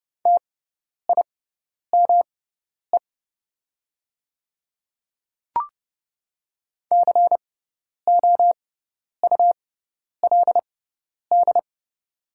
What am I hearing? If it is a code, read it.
Morse code, heard as a keyed tone, about 30 wpm, spaced as for 10 wpm, 700 Hz. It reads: TIME COULD
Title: 6x Character Spacing - Top 100 Words - 30wpm